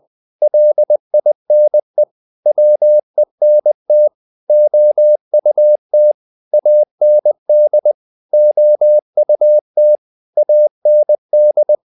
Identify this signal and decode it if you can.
Morse code, heard as a keyed tone, about 20 wpm, 600 Hz. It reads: LINE WENT OUT AND OUT AND
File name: the old man and the sea part 2 20wpm